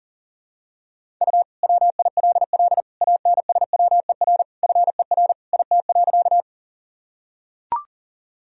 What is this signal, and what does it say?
Morse code, heard as a keyed tone, about 37 wpm, 700 Hz. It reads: U WILL ANSWER FER IT.
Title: Sentences from Top 300 Words and Abbreviations 40wpm